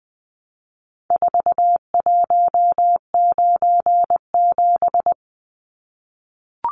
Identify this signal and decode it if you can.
Morse code, heard as a keyed tone, about 20 wpm, 700 Hz. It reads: 4197
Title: Numbers - 4-Digit 20wpm